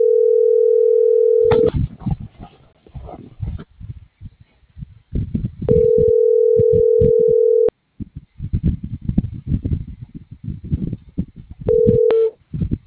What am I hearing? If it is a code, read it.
Ringback tone (North America), 440 + 480 Hz.